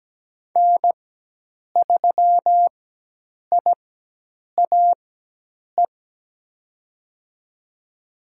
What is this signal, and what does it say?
Morse code, heard as a keyed tone, about 17 wpm, spaced as for 8 wpm, 700 Hz. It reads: N3IAE
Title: USA Call Signs - 5 Characters - 4x Character Spacing - 17wpm